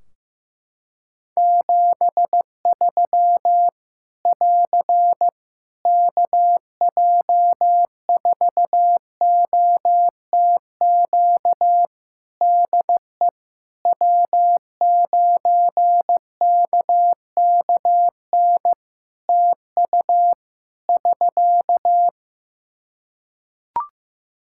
Morse code, heard as a keyed tone, about 15 wpm, 700 Hz. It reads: 73 <AR> KJ4OTQ DE W9KKN TU <SK>
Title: Practice QSO - Exchange 4 - Ending - 15wpm